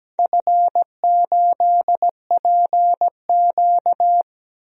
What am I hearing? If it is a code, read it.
Morse code, heard as a keyed tone, about 17 wpm, 700 Hz. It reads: F8PQ
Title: DXCC-100 Call Signs - 4 Characters - Repeated 3x - 17wpm